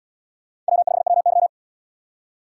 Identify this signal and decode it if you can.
Morse code, heard as a keyed tone, about 69 wpm, 700 Hz. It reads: W5WY